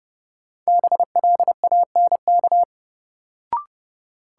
Morse code, heard as a keyed tone, about 30 wpm, 700 Hz. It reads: BLANK
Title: Sentences from Top 1500 Words - New Word Drill - 30wpm